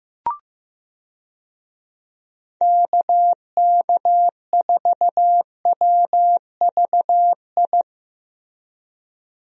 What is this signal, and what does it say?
Morse code, heard as a keyed tone, about 14 wpm, 700 Hz. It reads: KK4WVI
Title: USA Call Signs - 6 Characters - Repeated 3x - 15wpm